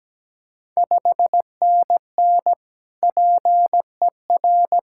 Morse code, heard as a keyed tone, about 17 wpm, 690 Hz. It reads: 5NN PER
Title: Ontario QSO Party Exchange 17wpm